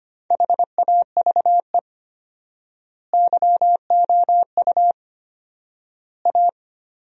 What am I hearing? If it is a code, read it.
Morse code, heard as a keyed tone, about 25 wpm, 700 Hz. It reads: HAVE YOU A
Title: 4x Word-Spacing - Sentences from Top 400 Words - 25wpm